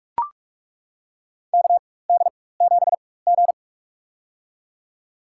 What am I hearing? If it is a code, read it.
Morse code, heard as a keyed tone, about 41 wpm, spaced as for 20 wpm, 700 Hz. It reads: KD7G
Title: USA Call Signs - 4 Characters - 4x Character Spacing - 45wpm